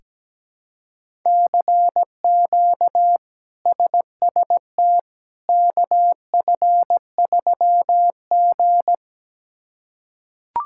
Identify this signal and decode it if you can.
Morse code, heard as a keyed tone, about 17 wpm, 700 Hz. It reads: CQ SST KF3G
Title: K1USN SST QSO Practice - 17wpm